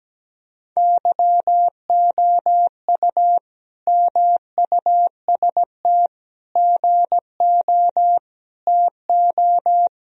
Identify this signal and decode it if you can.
Morse code, heard as a keyed tone, about 17 wpm, 700 Hz. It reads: YOU MUST GO TO
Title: Sentences from Top 1000 Words - Review Part 1 - 17wpm